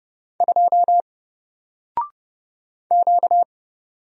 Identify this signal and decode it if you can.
Morse code, heard as a keyed tone, about 29 wpm, 700 Hz. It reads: 2 Q
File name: Single-Letter-Number-30wpm